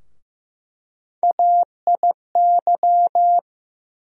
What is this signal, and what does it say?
Morse code, heard as a keyed tone, about 15 wpm, 700 Hz. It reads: AIY
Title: USA Call Sign Postfixes - 3 Letters 15wpm